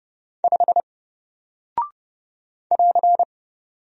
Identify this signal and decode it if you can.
Morse code, heard as a keyed tone, about 30 wpm, 700 Hz. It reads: 5 <AR>
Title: Single Character (Letter:Number:Punctuation:Prosign) Alt 30wpm